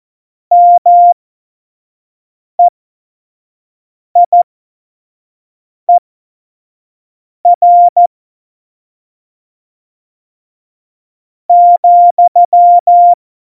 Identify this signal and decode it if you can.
Morse code, heard as a keyed tone, about 14 wpm, spaced as for 4 wpm, 700 Hz. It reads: MEIER,
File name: Morsekurs / Witze / Witz 01. September 2020, 25 F70